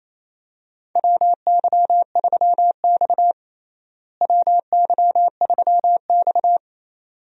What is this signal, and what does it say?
Morse code, heard as a keyed tone, about 27 wpm, 700 Hz. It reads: WY3X WY3X
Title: USA Call Signs - 4 Characters - Repeated 2x - 28wpm